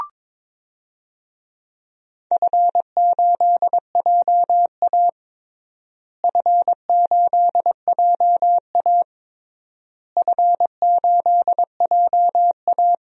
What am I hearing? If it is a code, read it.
Morse code, heard as a keyed tone, about 22 wpm, 700 Hz. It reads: F8JA F8JA F8JA
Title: DXCC-100 Call Signs - 4 Characters - Repeated 3x - 22wpm